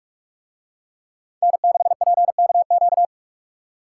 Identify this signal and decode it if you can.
Morse code, heard as a keyed tone, about 43 wpm, 700 Hz. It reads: N6PKQ